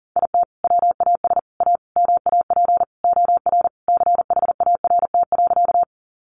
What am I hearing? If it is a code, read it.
Morse code, heard as a keyed tone, about 40 wpm, 700 Hz. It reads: IT WAS A MAP OR CHART.
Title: Sentences from Top 1300 Words 40wpm